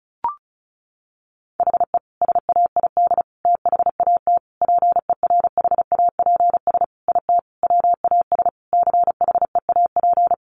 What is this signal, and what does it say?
Morse code, heard as a keyed tone, about 34 wpm, 700 Hz. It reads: HE SAID THAT PERHAPS IT WAS CHEAP
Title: Sentences from Top 1200 Words 35wpm